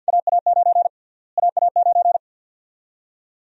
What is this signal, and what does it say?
Morse code, heard as a keyed tone, about 45 wpm, 700 Hz. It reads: AA9 AA9
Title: USA Call Sign Prefixes - Repeated 2x - 50wpm